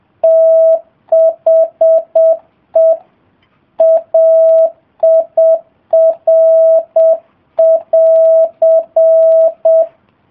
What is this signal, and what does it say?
Morse code, heard as a keyed tone, about 7 wpm, 650 Hz. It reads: THE AIR<AR>